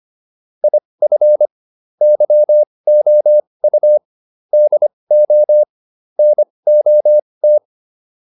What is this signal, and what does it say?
Morse code, heard as a keyed tone, about 25 wpm, spaced as for 20 wpm, 600 Hz. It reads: IF YOU DO NOT